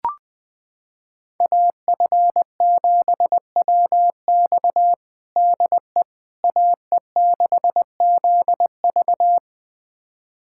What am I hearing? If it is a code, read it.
Morse code, heard as a keyed tone, about 20 wpm, 700 Hz. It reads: AF7WX DE AE6ZV
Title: USA Callsign DE Callsign - 5 Characters - 20wpm